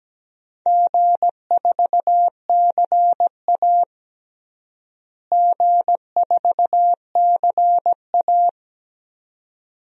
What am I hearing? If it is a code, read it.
Morse code, heard as a keyed tone, about 17 wpm, 700 Hz. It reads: G4CA G4CA
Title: DXCC-100 Call Signs - 4 Characters - Repeated 2x - 17wpm